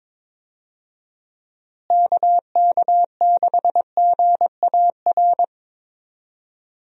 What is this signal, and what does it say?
Morse code, heard as a keyed tone, about 21 wpm, 700 Hz. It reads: KK6GAR
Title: USA Call Signs - 6 Characters - Repeated 2x - 22wpm